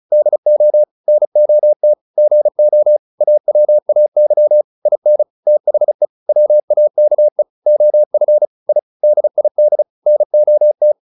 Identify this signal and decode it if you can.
Morse code, heard as a keyed tone, about 34 wpm, 600 Hz. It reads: DO NOT GO AWAY IN THE WAKE OF I DID NOT